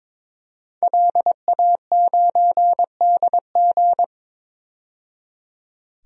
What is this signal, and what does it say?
Morse code, heard as a keyed tone, about 21 wpm, 700 Hz. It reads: LA9DG
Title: DXCC-100 Call Signs - 5 Characters - 22wpm